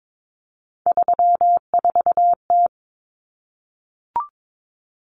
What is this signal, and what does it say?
Morse code, heard as a keyed tone, about 22 wpm, 700 Hz. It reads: 34T
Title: Common Cut Numbers - 3-Digit 22wpm